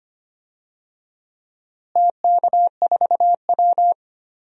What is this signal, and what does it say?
Morse code, heard as a keyed tone, about 25 wpm, 700 Hz. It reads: TK4W
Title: DXCC-100 Call Signs - 4 Characters - Repeated 2x - 25wpm